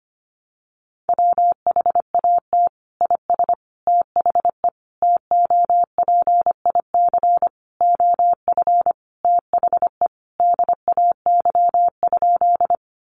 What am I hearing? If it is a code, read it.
Morse code, heard as a keyed tone, about 25 wpm, 700 Hz. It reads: WHAT IS THE TOPIC OF THE DAY?
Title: Sentences from Top 1200 Words 25wpm